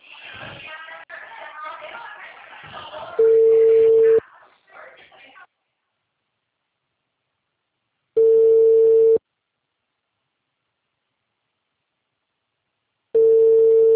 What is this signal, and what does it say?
Ringback tone (China, 450 Hz).